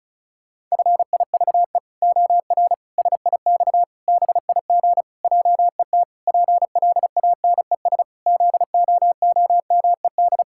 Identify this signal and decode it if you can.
Morse code, heard as a keyed tone, about 36 wpm, 700 Hz. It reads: FIVE OR SIX BIG JET PLANES ZOOMED
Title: Pangram Challenge v2 - 35wpm